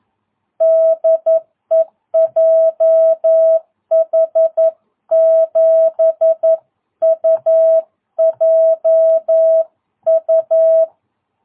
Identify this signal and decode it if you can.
Morse code, heard as a keyed tone, about 11 wpm, 650 Hz. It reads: DE JH7UJU
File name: jh8cbh / 2022cw